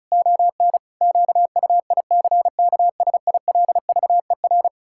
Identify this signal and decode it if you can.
Morse code, heard as a keyed tone, about 36 wpm, 700 Hz. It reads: ON QUICKSILVER
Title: Pangram Challenge v2 - 35wpm